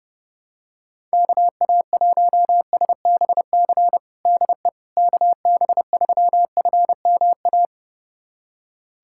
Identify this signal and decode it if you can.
Morse code, heard as a keyed tone, about 31 wpm, 700 Hz. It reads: KA1SBC DE KB3FMA
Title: USA Callsign DE Callsign - 6 Characters - 30wpm